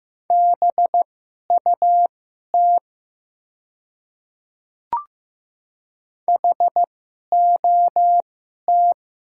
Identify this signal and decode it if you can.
Morse code, heard as a keyed tone, about 15 wpm, spaced as for 11 wpm, 700 Hz. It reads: BUT HOT
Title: 2x Character Spacing - Top 100 Words - 15wpm